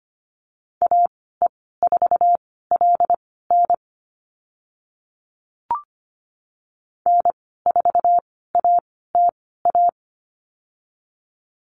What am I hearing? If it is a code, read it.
Morse code, heard as a keyed tone, about 25 wpm, spaced as for 16 wpm, 700 Hz. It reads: AE4LN N4ATA
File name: Lesson 15g - Number 5 - Call Signs 25wpm